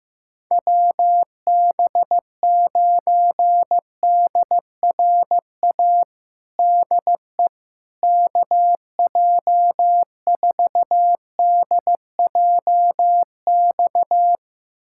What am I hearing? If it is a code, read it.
Morse code, heard as a keyed tone, about 15 wpm, 700 Hz. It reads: WB9DRA DE KJ4DJX
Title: USA Callsign DE Callsign - 6 Characters - 15wpm